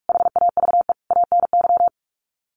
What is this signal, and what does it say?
Morse code, heard as a keyed tone, about 44 wpm, 700 Hz. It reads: HAVE ANY
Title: Sets of 5 Words Encore 45wpm